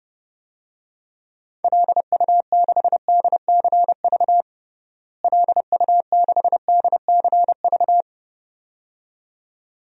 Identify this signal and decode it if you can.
Morse code, heard as a keyed tone, about 30 wpm, 700 Hz. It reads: LU6DCV LU6DCV
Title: DXCC-100 Call Signs - 6 Characters - Repeated 2x - 30wpm